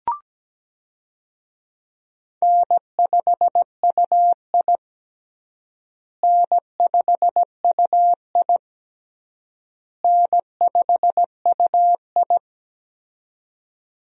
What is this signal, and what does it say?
Morse code, heard as a keyed tone, about 17 wpm, 700 Hz. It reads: N5UI N5UI N5UI
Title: USA Call Signs - 4 Characters - Repeated 3x - 17wpm